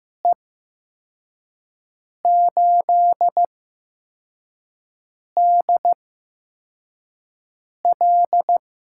Morse code, heard as a keyed tone, about 15 wpm, spaced as for 4 wpm, 700 Hz. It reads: E8DL
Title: USA Call Signs - 8x Character Spacing - 15wpm